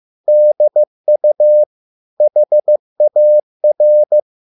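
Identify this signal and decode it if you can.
Morse code, heard as a keyed tone, about 15 wpm, 600 Hz. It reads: DU HAR
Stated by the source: Morse code at a slow 15 wpm from CW med Gnister.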